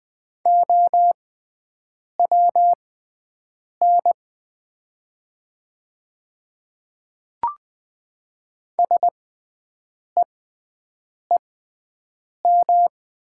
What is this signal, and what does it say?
Morse code, heard as a keyed tone, about 20 wpm, spaced as for 7 wpm, 700 Hz. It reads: OWN SEEM
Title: 6x Character Spacing - Top 300 Words - 20wpm